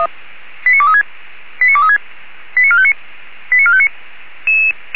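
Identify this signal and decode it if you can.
Five-tone selective call (ZVEI), heard as quick runs of five tones, tones of about 70 ms. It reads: 89236 89236 89459 89459
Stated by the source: ZVEI five-tone call received by SDR.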